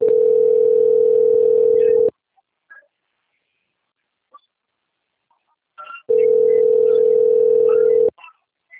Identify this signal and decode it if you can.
Ringback tone (North America), 440 + 480 Hz.